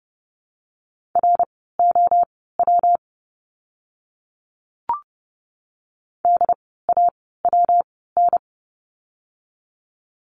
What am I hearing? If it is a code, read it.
Morse code, heard as a keyed tone, about 30 wpm, spaced as for 17 wpm, 700 Hz. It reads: ROW DAWN
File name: course / 30:10wpm / 115 Lesson 19f - Letter w - Words 30wpm